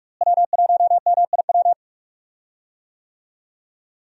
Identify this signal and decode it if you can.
Morse code, heard as a keyed tone, about 42 wpm, 700 Hz. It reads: W1MIW
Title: USA Call Signs - 5 Characters - Repeated 3x - 45wpm